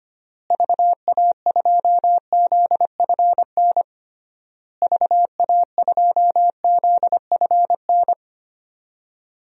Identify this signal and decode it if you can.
Morse code, heard as a keyed tone, about 25 wpm, 700 Hz. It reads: VA2ZFN VA2ZFN
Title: DXCC-100 Call Signs - 6 Characters - Repeated 3x - 25wpm